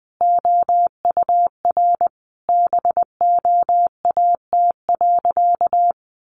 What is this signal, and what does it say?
Morse code, heard as a keyed tone, about 20 wpm, 700 Hz. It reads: OUR BOAT.
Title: Sentences from Top 1400 Words 20wpm